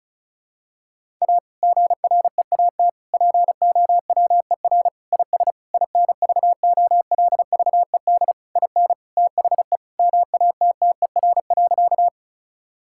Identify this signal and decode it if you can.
Morse code, heard as a keyed tone, about 36 wpm, 700 Hz. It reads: A GREAT POWER IS INVOLVED IN THE MATTER.
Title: Sentences from Top 1000 Words - Review Part 5 - 35wpm